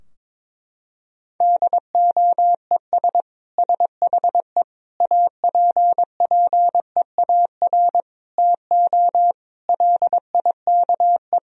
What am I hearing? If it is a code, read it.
Morse code, heard as a keyed tone, about 22 wpm, 700 Hz. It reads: DOES SHE APPEAR TO LIKE
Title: Sentences from Top 1000 Words - Review Part 2 - 22wpm